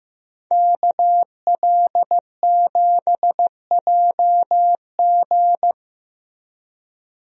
Morse code, heard as a keyed tone, about 15 wpm, 700 Hz. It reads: KL7JG
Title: DXCC-100 Call Signs - 5 Characters - Repeated 2x - 15wpm